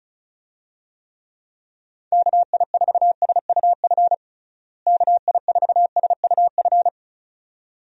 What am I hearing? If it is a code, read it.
Morse code, heard as a keyed tone, about 36 wpm, 700 Hz. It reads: KI4SUF KI4SUF
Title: USA Call Signs - 6 Characters - Repeated 2x - 35wpm